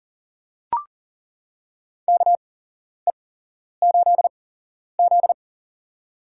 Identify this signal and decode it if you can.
Morse code, heard as a keyed tone, about 38 wpm, spaced as for 11 wpm, 700 Hz. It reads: KE8Z
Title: USA Call Signs - 8x Character Spacing - 40wpm